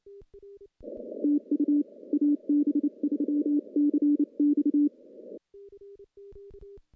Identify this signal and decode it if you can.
Morse code, heard as a keyed tone, about 29 wpm, 290 Hz. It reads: TU AB3CX
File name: contests / CW-SS-k5tr- -1300